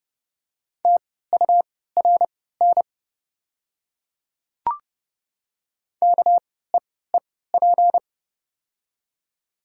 Morse code, heard as a keyed tone, about 30 wpm, spaced as for 17 wpm, 700 Hz. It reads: TURN KEEP
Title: 3x Character Spacing - Top 300 Words - 30wpm